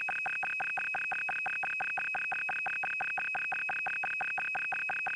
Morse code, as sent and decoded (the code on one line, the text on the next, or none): none